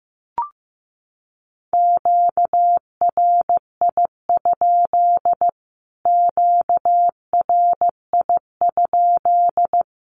Morse code, heard as a keyed tone, about 14 wpm, 700 Hz. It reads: QRI? QRI?
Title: Q-Codes Repeated 3x 15wpm